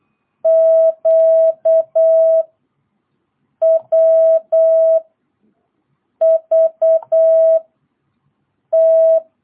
Morse code, heard as a keyed tone, about 8 wpm, 660 Hz. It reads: Q W V T
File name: jh8cbh / 2022cw